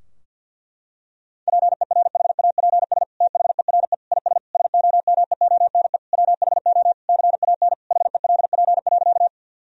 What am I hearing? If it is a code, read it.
Morse code, heard as a keyed tone, about 50 wpm, 700 Hz. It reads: PERHAPS THERE IS SOMEONE WHO CAN HELP.
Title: Sentences from Top 1000 Words - Review Part 4 - 50wpm